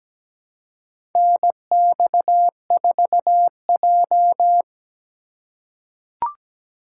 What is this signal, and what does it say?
Morse code, heard as a keyed tone, about 17 wpm, 700 Hz. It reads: NX4J
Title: USA Call Signs - 4 Characters - 3x Character Spacing - 17wpm